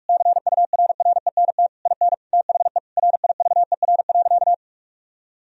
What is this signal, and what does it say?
Morse code, heard as a keyed tone, about 41 wpm, 700 Hz. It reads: CURRENT IN THE RIVER.
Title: Sentences from Top 1000 Words - Review Part 4 - 45wpm